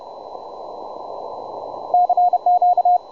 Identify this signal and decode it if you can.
Morse code, heard as a keyed tone, about 30 wpm, 720 Hz. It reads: CQ
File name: juma-trx2 / juma-trx2-20m-cw-sample